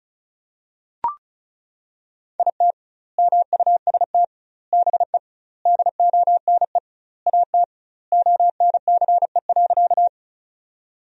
Morse code, heard as a keyed tone, about 36 wpm, 700 Hz. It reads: IT MUST BE DONE AT ONCE.
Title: 2x Word-Spacing - Sentences from Top 500 Words - 35wpm